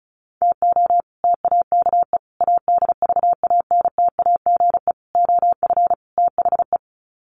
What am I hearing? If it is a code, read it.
Morse code, heard as a keyed tone, about 34 wpm, 700 Hz. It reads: TO TAKE ADVANTAGE OF THE
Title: Sets of 5 Words 35wpm